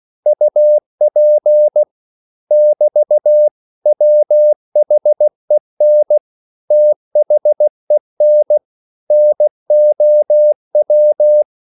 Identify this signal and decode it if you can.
Morse code, heard as a keyed tone, about 16 wpm, 600 Hz. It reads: UP <BT> WHEN THEN NOW
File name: CW / Practice / 500MostCommonEnglishWords16